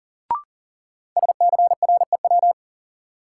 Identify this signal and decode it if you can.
Morse code, heard as a keyed tone, about 40 wpm, 700 Hz. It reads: SCREW
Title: Sentences from Top 1600 Words - New Word Drill - 40wpm